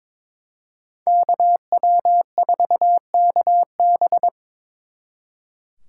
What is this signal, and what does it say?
Morse code, heard as a keyed tone, about 26 wpm, 700 Hz. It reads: KW4KB
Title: Speed Racing - USA Call Signs 22wpm